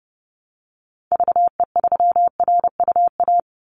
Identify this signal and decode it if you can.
Morse code, heard as a keyed tone, about 30 wpm, 700 Hz. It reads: VE3RUA
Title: Call Signs - 30wpm